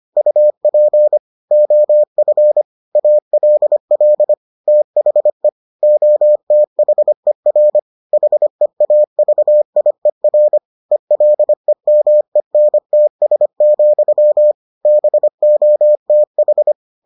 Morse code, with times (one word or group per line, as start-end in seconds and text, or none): 0.2-1.2 UP
1.5-2.6 OF
3.0-4.3 ALL
4.7-5.5 THE
5.8-7.8 OTHER
8.1-10.6 HEAVIER
10.9-14.5 ELEMENTS,
14.9-16.7 BOTH